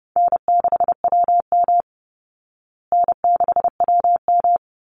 Morse code, heard as a keyed tone, about 30 wpm, 700 Hz. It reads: N6WM N6WM
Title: Easy Call Signs Repeated 2x - 30wpm